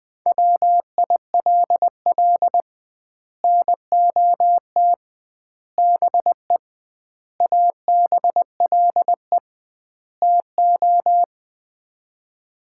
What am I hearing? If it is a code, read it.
Morse code, heard as a keyed tone, about 21 wpm, 700 Hz. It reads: WILL NOT BE ABLE TO
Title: 2x Word-Spacing - Sets of 5 Words - 20wpm